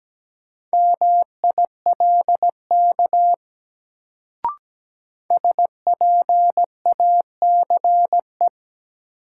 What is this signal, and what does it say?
Morse code, heard as a keyed tone, about 17 wpm, 700 Hz. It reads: MILK SPACE
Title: Sentences from Top 1500 Words - New Word Drill Alt - 17wpm